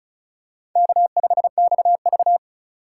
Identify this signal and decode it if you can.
Morse code, heard as a keyed tone, about 36 wpm, 700 Hz. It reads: K5XV
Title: USA Call Signs - 4 Characters - 2x Character Spacing - 35wpm